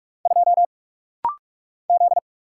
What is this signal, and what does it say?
Morse code, heard as a keyed tone, about 44 wpm, 700 Hz. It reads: 2 Z
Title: Single Letter-Number Alt - Rapid-Fire 45wpm